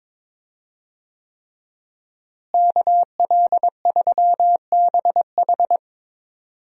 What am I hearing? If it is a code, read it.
Morse code, heard as a keyed tone, about 22 wpm, 700 Hz. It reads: KL3BH
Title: USA Call Signs - 5 Characters - Repeated 3x - 22wpm